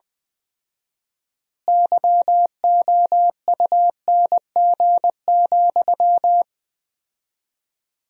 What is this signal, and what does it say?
Morse code, heard as a keyed tone, about 21 wpm, 700 Hz. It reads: YOUNG,
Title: 4x Word-Spacing - Sentences from Top 400 Words - 20wpm